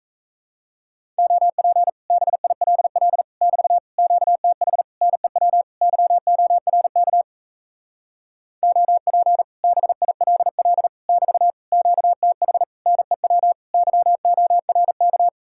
Morse code, heard as a keyed tone, about 37 wpm, 700 Hz. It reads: OP BILL <BT> QTH NEW YORK OP BILL <BT> QTH NEW YORK